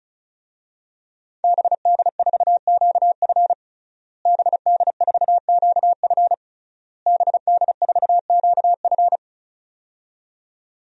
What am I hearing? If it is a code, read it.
Morse code, heard as a keyed tone, about 34 wpm, 700 Hz. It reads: BD4QF BD4QF BD4QF
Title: DXCC-100 Call Signs - 5 Characters - Repeated 3x - 35wpm